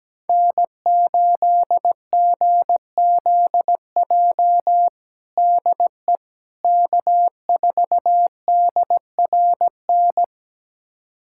Morse code, heard as a keyed tone, about 17 wpm, 700 Hz. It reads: N8GZJ DE K4DRN
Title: USA Callsign DE Callsign 17wpm